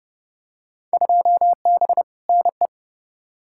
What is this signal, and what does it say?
Morse code, heard as a keyed tone, about 30 wpm, 700 Hz. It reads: 2B NE
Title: Field Day Exchanges Encore - 30wpm